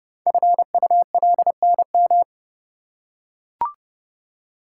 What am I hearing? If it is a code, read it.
Morse code, heard as a keyed tone, about 31 wpm, 690 Hz. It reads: FULNM
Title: Code Groups - 5 Letters - 30wpm